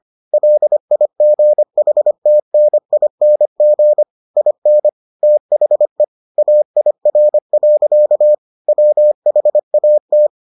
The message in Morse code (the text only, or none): LIGHTNING IN THE AIR. WHAT